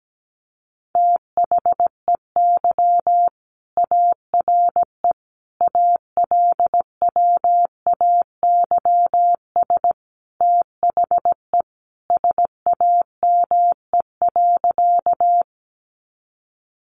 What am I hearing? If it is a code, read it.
Morse code, heard as a keyed tone, about 17 wpm, 700 Hz. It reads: THEY ARE ALWAYS THE SAME.
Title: Sentences from Top 300 Words 17wpm